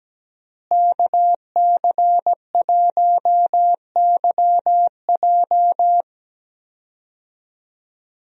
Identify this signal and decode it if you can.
Morse code, heard as a keyed tone, about 17 wpm, 700 Hz. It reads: KC1YJ
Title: USA Call Signs - 5 Characters - Repeated 3x - 17wpm